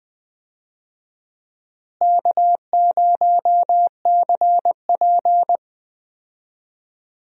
Morse code, heard as a keyed tone, about 20 wpm, 700 Hz. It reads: K0CP
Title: USA Call Signs - 4 Characters - Repeated 2x - 20wpm